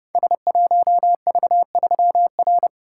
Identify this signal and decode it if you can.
Morse code, heard as a keyed tone, about 30 wpm, 700 Hz. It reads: S1V3R
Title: Code Groups - 5 Letter:Numbers - 30wpm